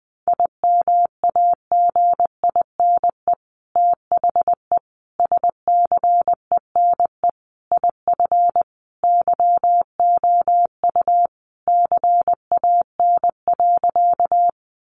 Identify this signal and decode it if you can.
Morse code, heard as a keyed tone, about 20 wpm, 700 Hz. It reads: IMAGINE THE SCENE IF YOU CAN.
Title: Sentences from Top 1400 Words 20wpm